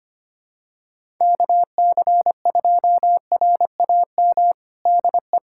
Morse code, heard as a keyed tone, about 25 wpm, 700 Hz. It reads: KC2RAM DE
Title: USA Callsign DE Callsign - 6 Characters - 25wpm